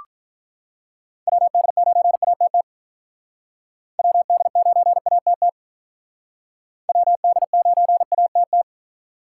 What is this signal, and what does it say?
Morse code, heard as a keyed tone, about 45 wpm, 700 Hz. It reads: WD9ATT WD9ATT WD9ATT